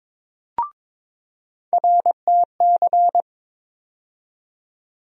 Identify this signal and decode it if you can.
Morse code, heard as a keyed tone, about 22 wpm, 700 Hz. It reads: RTC
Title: USA Call Sign Postfixes - 3 Letters 22wpm